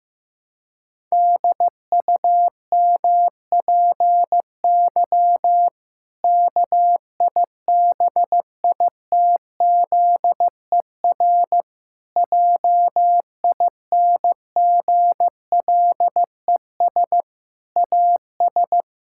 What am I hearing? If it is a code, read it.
Morse code, heard as a keyed tone, about 15 wpm, 700 Hz. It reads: DUMPY KIBITZER JINGLES AS
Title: Pangram Challenge v2 - 15wpm